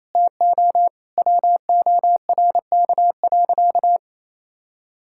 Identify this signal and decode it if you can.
Morse code, heard as a keyed tone, about 27 wpm, 700 Hz. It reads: TO WORK.
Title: Sentences from Top 1000 Words - Review Part 6 - 28wpm